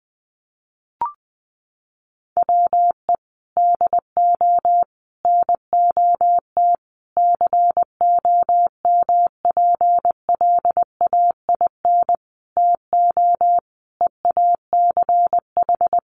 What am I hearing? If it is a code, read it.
Morse code, heard as a keyed tone, about 20 wpm, 700 Hz. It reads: WE DO NOT COMPLAIN TO EACH
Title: Sentences from Top 1300 Words 20wpm